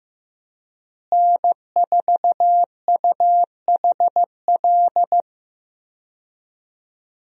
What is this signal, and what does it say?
Morse code, heard as a keyed tone, about 16 wpm, 700 Hz. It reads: N4UHL
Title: USA Call Signs - 5 Characters - Repeated 3x - 15wpm